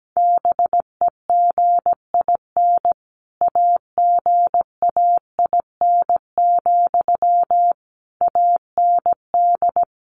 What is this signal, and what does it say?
Morse code, heard as a keyed tone, about 17 wpm, 700 Hz. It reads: BEGIN AGAIN, AND
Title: Sentences from Top 800 Words 17wpm